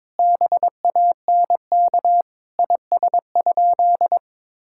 Morse code, heard as a keyed tone, about 22 wpm, 700 Hz. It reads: BANK IS?
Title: Sentences from Top 1000 Words - Review Part 6 - 22wpm